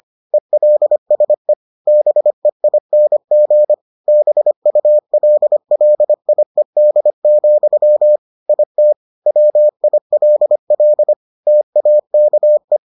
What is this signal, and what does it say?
Morse code, heard as a keyed tone, about 25 wpm, 600 Hz. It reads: ELSE BEING BULLIED, IT WILL TAKE